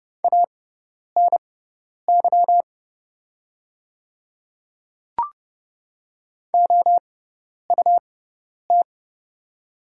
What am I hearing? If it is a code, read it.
Morse code, heard as a keyed tone, about 30 wpm, spaced as for 10 wpm, 700 Hz. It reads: ANY OUT
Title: 6x Character Spacing - Top 200 Words - 30wpm